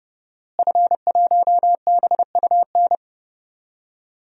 Morse code, heard as a keyed tone, about 30 wpm, 700 Hz. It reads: F1BUN
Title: DXCC-100 Call Signs - 5 Characters - Repeated 3x - 30wpm